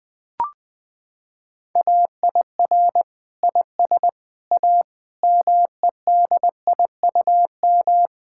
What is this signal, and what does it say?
Morse code, heard as a keyed tone, about 21 wpm, 700 Hz. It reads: AIR IS A MEDIUM
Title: Sentences from Top 1000 Words - Review Part 6 - 20wpm